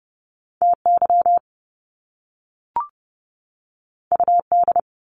Morse code, heard as a keyed tone, about 30 wpm, 700 Hz. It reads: TY UD
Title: Top Two Letter Combinations in Words 30wpm